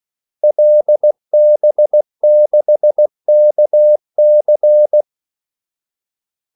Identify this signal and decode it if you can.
Morse code, heard as a keyed tone, about 16 wpm, 600 Hz. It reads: LB6KC
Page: Kaldesignaler 16 wpm